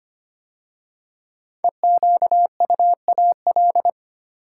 Morse code, heard as a keyed tone, about 25 wpm, 700 Hz. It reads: EQUAL
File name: 5 Letter Words - First 3 Letters Given - 25wpm